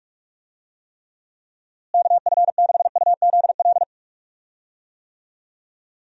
Morse code, heard as a keyed tone, about 41 wpm, 700 Hz. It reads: KF6UZL
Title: USA Call Signs - 6 Characters - Repeated 2x - 45wpm